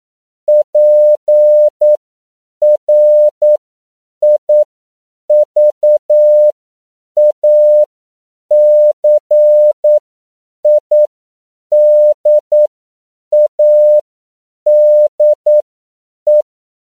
Morse code, heard as a keyed tone, about 9 wpm, spaced as for 7 wpm, 600 Hz. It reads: PRIVACIDADE
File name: CW / 10PPM